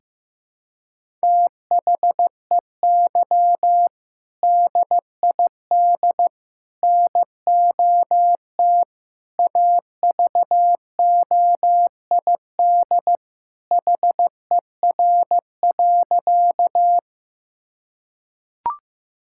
Morse code, heard as a keyed tone, about 15 wpm, 700 Hz. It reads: THEY DID NOT AVOID HER.
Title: Sentences from Top 1000 Words - Review Part 6 - 15wpm